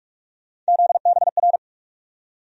Morse code, heard as a keyed tone, about 46 wpm, 700 Hz. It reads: ZBR
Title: USA Call Sign Postfixes - 3 Letters 45wpm